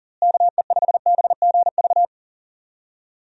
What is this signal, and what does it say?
Morse code, heard as a keyed tone, about 40 wpm, 700 Hz. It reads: KE5BGV
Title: USA Call Signs - 6 Characters - Repeated 2x - 40wpm